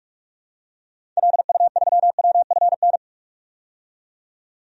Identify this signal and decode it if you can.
Morse code, heard as a keyed tone, about 46 wpm, 700 Hz. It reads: LU3WFN
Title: DXCC-100 Call Signs - 6 Characters - 45wpm